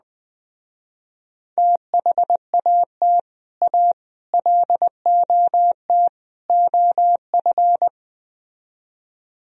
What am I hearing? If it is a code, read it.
Morse code, heard as a keyed tone, about 20 wpm, 700 Hz. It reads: THAT A LOT OF